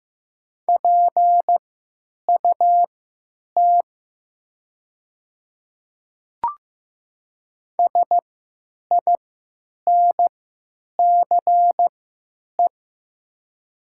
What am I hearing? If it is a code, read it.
Morse code, heard as a keyed tone, about 15 wpm, spaced as for 8 wpm, 700 Hz. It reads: PUT SINCE
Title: 3x Character Spacing - Top 300 Words - 15wpm